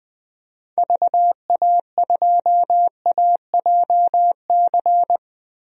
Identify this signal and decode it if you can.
Morse code, heard as a keyed tone, about 20 wpm, 700 Hz. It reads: VA2AJC
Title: DXCC-100 Call Signs - 6 Characters - 20wpm